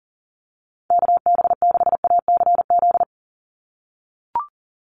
Morse code, heard as a keyed tone, about 40 wpm, 690 Hz. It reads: KB6ACZ
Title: USA Call Signs 40wpm